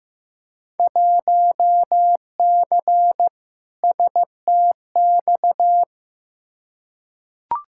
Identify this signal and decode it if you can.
Morse code, heard as a keyed tone, about 16 wpm, 700 Hz. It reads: 1C STX
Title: Field Day Exchanges Encore - 15wpm